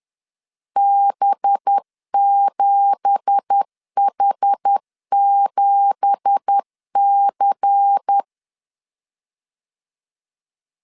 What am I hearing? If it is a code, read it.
Morse code, heard as a keyed tone, about 10 wpm, 800 Hz. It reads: B7H7C
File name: LEZIONI / 9 2867 10wpm 2733